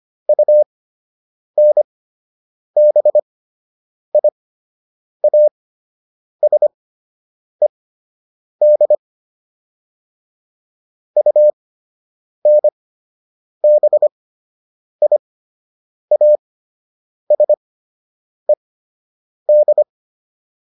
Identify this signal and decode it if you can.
Morse code, heard as a keyed tone, about 25 wpm, spaced as for 8 wpm, 600 Hz. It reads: UNBIASED UNBIASED